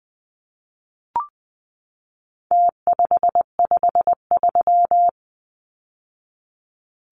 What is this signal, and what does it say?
Morse code, heard as a keyed tone, about 20 wpm, 700 Hz. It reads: T553
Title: Numbers - Contest Style - 4-Digit 20wpm